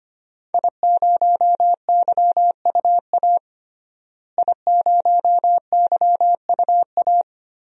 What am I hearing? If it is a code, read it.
Morse code, heard as a keyed tone, about 25 wpm, 700 Hz. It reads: I0YUA I0YUA
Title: DXCC-100 Call Signs - 5 Characters - Repeated 3x - 25wpm